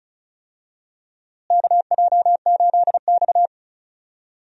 Morse code, heard as a keyed tone, about 36 wpm, 700 Hz. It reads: KJ8X
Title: USA Call Signs - 4 Characters - Repeated 3x - 35wpm